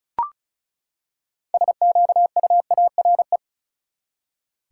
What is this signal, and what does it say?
Morse code, heard as a keyed tone, about 34 wpm, 700 Hz. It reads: SQUARE
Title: Six Letter Words - 35wpm